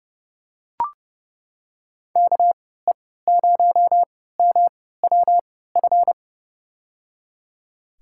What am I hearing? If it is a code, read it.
Morse code, heard as a keyed tone, about 30 wpm, spaced as for 17 wpm, 700 Hz. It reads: KE0MWF
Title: USA Call Signs - 6 Characters - 3x Character Spacing - 30wpm